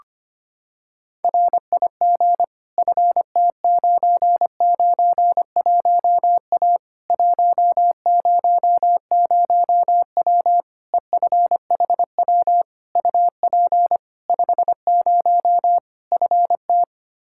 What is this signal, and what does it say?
Morse code, heard as a keyed tone, about 25 wpm, 700 Hz. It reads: RIG FT991A 100W EFHW UP 50 FT
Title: Practice QSO - Exchange 2 - Info - 25wpm